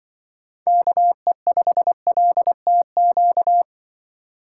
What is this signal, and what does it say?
Morse code, heard as a keyed tone, about 25 wpm, 700 Hz. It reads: KE5LTQ
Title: Speed Racing - USA 2x3 Call Signs 20wpm